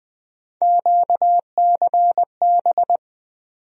Morse code, heard as a keyed tone, about 20 wpm, 700 Hz. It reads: QCB
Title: USA Call Sign Postfixes - 3 Letters 20wpm